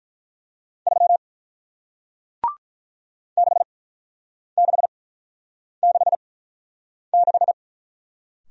Morse code, heard as a keyed tone, about 46 wpm, spaced as for 9 wpm, 700 Hz. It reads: 3 6666